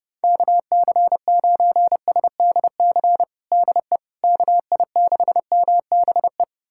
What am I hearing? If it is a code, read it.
Morse code, heard as a keyed tone, about 32 wpm, 700 Hz. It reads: KC9SDC DE KI6MBE
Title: USA Callsign DE Callsign - 6 Characters - 30wpm